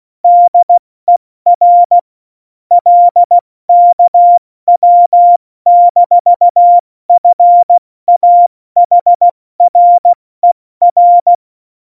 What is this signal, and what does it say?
Morse code, heard as a keyed tone, about 16 wpm, 700 Hz. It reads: DER LKW-FAHRER
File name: Morsekurs / Witze / Witz 25. Februar 2020, 70 F80